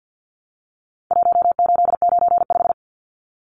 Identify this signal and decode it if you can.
Morse code, heard as a keyed tone, about 50 wpm, 700 Hz. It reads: 1895
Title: Numbers - Contest Style - 4-Digit 50wpm